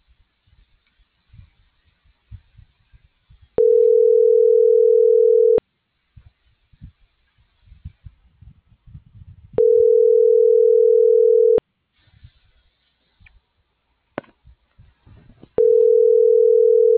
Ringback tone (North America), 440 + 480 Hz.